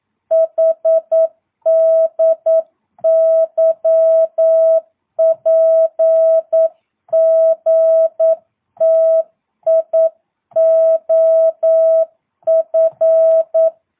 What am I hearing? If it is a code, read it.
Morse code, heard as a keyed tone, about 9 wpm, 650 Hz. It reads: HDYPGTIOF